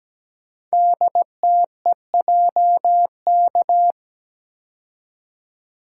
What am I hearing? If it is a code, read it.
Morse code, heard as a keyed tone, about 17 wpm, 700 Hz. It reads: DTEJK
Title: Code Groups - 5 Letters - 17wpm